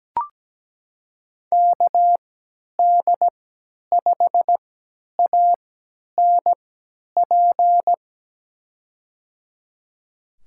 Morse code, heard as a keyed tone, about 17 wpm, spaced as for 10 wpm, 700 Hz. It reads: KD5ANP
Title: USA Call Signs - 6 Characters - 3x Character Spacing - 17wpm